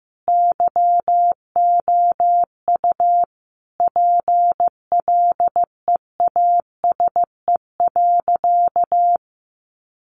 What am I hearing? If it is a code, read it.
Morse code, heard as a keyed tone, about 15 wpm, 700 Hz. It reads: YOU PLEASE.
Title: Sentences from Top 1500 Words 15wpm